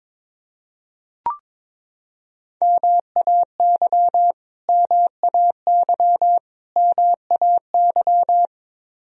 Morse code, heard as a keyed tone, about 22 wpm, 700 Hz. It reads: MAY MAY MAY
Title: Top 100 Words Repeated 22wpm